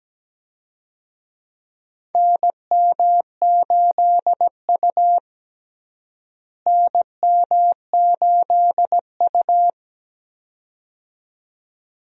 Morse code, heard as a keyed tone, about 17 wpm, 700 Hz. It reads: NM8U NM8U
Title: USA Call Signs - 4 Characters - Repeated 2x - 17wpm